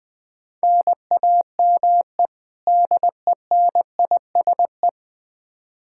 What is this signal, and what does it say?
Morse code, heard as a keyed tone, about 20 wpm, 700 Hz. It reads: NAME DENISE
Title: Practice QSO - Exchange 1 - Beginning - 20wpm